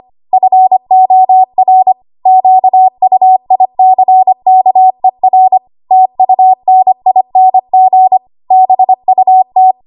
Morse code, heard as a keyed tone, about 25 wpm, 750 Hz. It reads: FOR QUICKER TUNING BUT
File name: Archive / 25 WPM / 25WPM